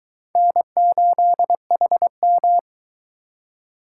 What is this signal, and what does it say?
Morse code, heard as a keyed tone, about 22 wpm, 700 Hz. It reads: N8HM